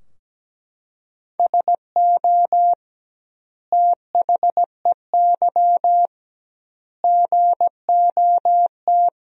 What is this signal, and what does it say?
Morse code, heard as a keyed tone, about 17 wpm, 700 Hz. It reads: SO THEY GOT
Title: 2x Word-Spacing - Sentences from Top 300 Words - 17wpm